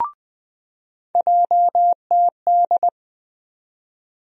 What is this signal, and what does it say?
Morse code, heard as a keyed tone, about 20 wpm, 700 Hz. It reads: JTD
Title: USA Call Sign Postfixes - 3 Letters 20wpm